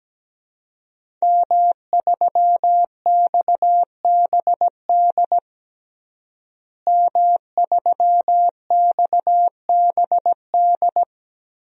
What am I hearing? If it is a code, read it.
Morse code, heard as a keyed tone, about 17 wpm, 700 Hz. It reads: M3XBD M3XBD
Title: DXCC-100 Call Signs - 5 Characters - Repeated 3x - 17wpm